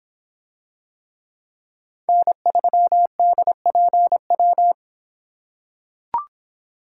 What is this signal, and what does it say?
Morse code, heard as a keyed tone, about 25 wpm, 700 Hz. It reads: N3DPW